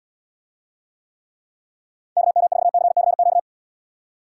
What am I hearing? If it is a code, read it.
Morse code, heard as a keyed tone, about 78 wpm, 700 Hz. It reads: KM4CCQ